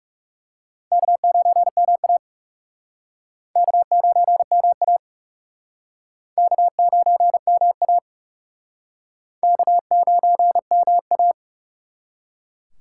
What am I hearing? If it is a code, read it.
Morse code, heard as a keyed tone, about 37 wpm, 700 Hz. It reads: K9MA K9MA K9MA K9MA
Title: Speed Racing - USA Call Signs 30wpm